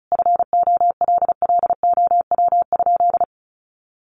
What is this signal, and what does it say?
Morse code, heard as a keyed tone, about 34 wpm, 700 Hz. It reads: FOLLOW?
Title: Sentences from Top 1100 Words 35wpm